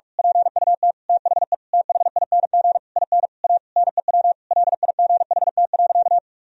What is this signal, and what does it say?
Morse code, heard as a keyed tone, about 46 wpm, 700 Hz. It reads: PUT THE THING IN A NEW LIGHT.
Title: Sentences from Top 1000 Words - Review Part 1 - 45wpm